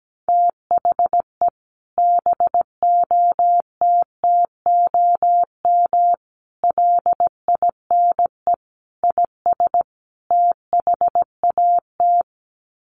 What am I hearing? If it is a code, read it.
Morse code, heard as a keyed tone, about 17 wpm, 700 Hz. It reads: THE BOTTOM LINE IS THAT
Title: Sets of 5 Words Encore 17wpm